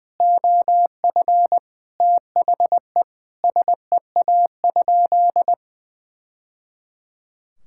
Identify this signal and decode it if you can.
Morse code, heard as a keyed tone, about 20 wpm, 700 Hz. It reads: OF THE SEA?
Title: Sentences from Top 1000 Words - Review Part 5 - 20wpm